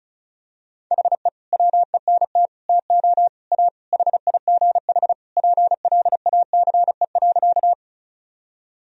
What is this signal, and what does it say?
Morse code, heard as a keyed tone, about 36 wpm, 700 Hz. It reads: HE WENT TO A HIGH PLACE.
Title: Sentences from Top 1000 Words - Review Part 1 - 35wpm